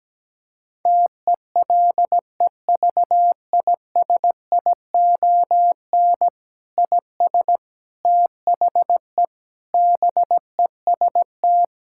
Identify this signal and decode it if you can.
Morse code, heard as a keyed tone, about 17 wpm, 700 Hz. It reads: TELEVISION IS THE BEST
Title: Sentences from Top 1000 Words - Review Part 6 - 17wpm